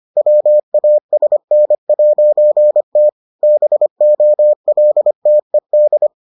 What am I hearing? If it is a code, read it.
Morse code, heard as a keyed tone, about 25 wpm, 600 Hz. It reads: WASN'T BOLTED